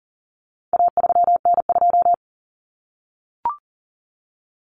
Morse code, heard as a keyed tone, about 40 wpm, 700 Hz. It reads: A3N2
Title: Common Cut Numbers - 4-Digit 40wpm